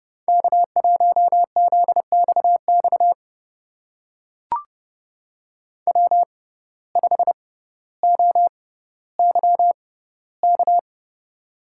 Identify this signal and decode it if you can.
Morse code, heard as a keyed tone, about 30 wpm, spaced as for 15 wpm, 700 Hz. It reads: K1ZXX W5OYK
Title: USA Call Signs - 6x Character Spacing - 30wpm